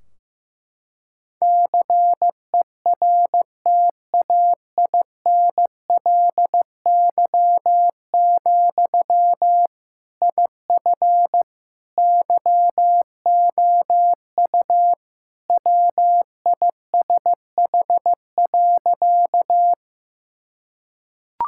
Morse code, heard as a keyed tone, about 16 wpm, 700 Hz. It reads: CERTAINLY, IF YOU WISH.
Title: Sentences from Top 1600 Words 15wpm